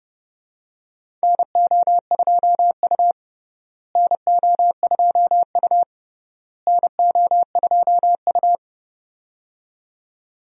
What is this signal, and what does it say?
Morse code, heard as a keyed tone, about 30 wpm, 700 Hz. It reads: NO2U NO2U NO2U
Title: USA Call Signs - 4 Characters - Repeated 3x - 30wpm